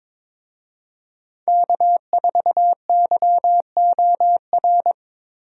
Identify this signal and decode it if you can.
Morse code, heard as a keyed tone, about 21 wpm, 700 Hz. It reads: K4YOR
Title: USA Call Signs - 5 Characters - Repeated 2x - 22wpm